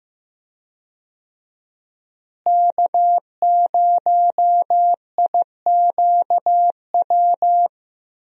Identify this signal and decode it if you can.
Morse code, heard as a keyed tone, about 16 wpm, 700 Hz. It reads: K0IQW
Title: USA Call Signs - 5 Characters - Repeated 2x - 15wpm